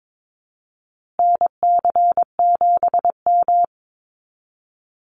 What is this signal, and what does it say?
Morse code, heard as a keyed tone, about 22 wpm, 700 Hz. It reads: NC7M
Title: Call Signs Repeated 3x - 22wpm